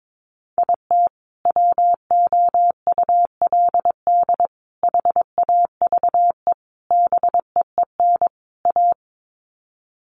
Morse code, heard as a keyed tone, about 22 wpm, 700 Hz. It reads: IT WOULD HAVE BEEN A
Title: Sets of 5 Words Encore 22wpm